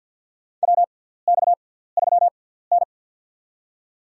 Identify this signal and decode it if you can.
Morse code, heard as a keyed tone, about 52 wpm, spaced as for 17 wpm, 700 Hz. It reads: WX3N
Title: USA Call Signs - 6x Character Spacing - 50wpm